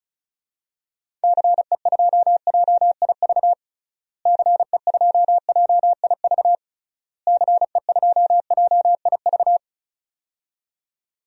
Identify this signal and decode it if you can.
Morse code, heard as a keyed tone, about 36 wpm, 700 Hz. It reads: CE2JIV CE2JIV CE2JIV
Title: DXCC-100 Call Signs - 6 Characters - Repeated 3x - 35wpm